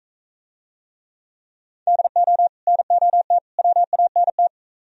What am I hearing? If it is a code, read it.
Morse code, heard as a keyed tone, about 41 wpm, 700 Hz. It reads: DO NOT WANT